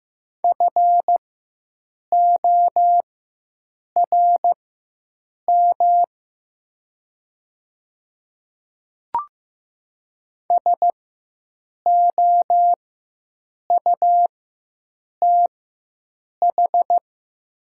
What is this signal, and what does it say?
Morse code, heard as a keyed tone, about 15 wpm, spaced as for 7 wpm, 700 Hz. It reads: FORM SOUTH
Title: 4x Character Spacing - Top 500 Words - 15wpm